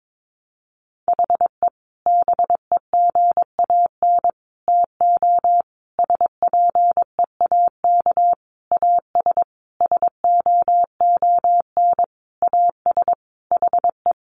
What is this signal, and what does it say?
Morse code, heard as a keyed tone, about 22 wpm, 700 Hz. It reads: HE BEGAN TO SPEAK AS SOON AS HE
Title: Sentences from Top 1100 Words 22wpm